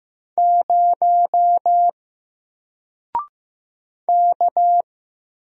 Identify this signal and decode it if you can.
Morse code, heard as a keyed tone, about 15 wpm, 700 Hz. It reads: 0 K
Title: Single Character (Letter:Number:Punctuation:Prosign) 15wpm